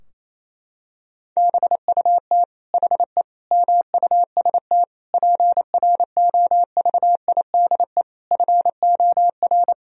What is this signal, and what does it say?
Morse code, heard as a keyed tone, about 27 wpm, 700 Hz. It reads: BUT HE MUST PROVIDE FOR
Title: Sentences from Top 1000 Words - Review Part 3 - 28wpm